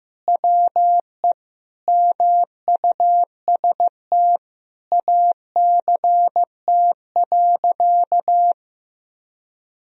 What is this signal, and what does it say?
Morse code, heard as a keyed tone, about 15 wpm, 700 Hz. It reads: WE MUST ACT.
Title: Sentences from Top 1000 Words - Review Part 1 - 15wpm